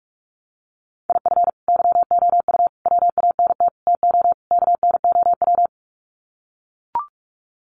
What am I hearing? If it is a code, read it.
Morse code, heard as a keyed tone, about 44 wpm, 700 Hz. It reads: IF YOU WANT TO KNOW
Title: Sets of 5 Words Encore2 45wpm